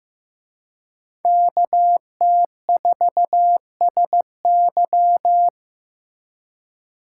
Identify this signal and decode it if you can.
Morse code, heard as a keyed tone, about 18 wpm, 700 Hz. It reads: KT4SY
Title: Speed Racing - USA 2x2 Call Signs 15wpm